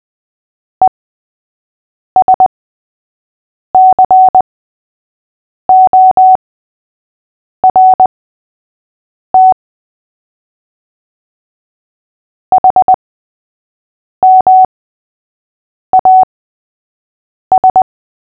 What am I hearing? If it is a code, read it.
Morse code, heard as a keyed tone, about 20 wpm, spaced as for 6 wpm, 750 Hz. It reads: ESCORT HMAS
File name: cw / fists / Short-Story-3-6wpm